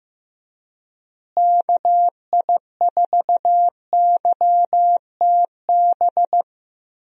Morse code, heard as a keyed tone, about 16 wpm, 700 Hz. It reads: KI4YTB
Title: USA Call Signs - 6 Characters - Repeated 2x - 15wpm